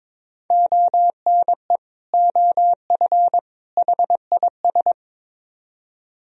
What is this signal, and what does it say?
Morse code, heard as a keyed tone, about 22 wpm, 700 Hz. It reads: ONE OF HIS
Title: Speed-Racing - Sets of 3 Words 22wpm